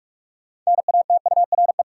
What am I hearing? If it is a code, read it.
Morse code, heard as a keyed tone, about 41 wpm, 700 Hz. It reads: NATURE
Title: Sentences from Top 600 Words - New Word Drill - 45wpm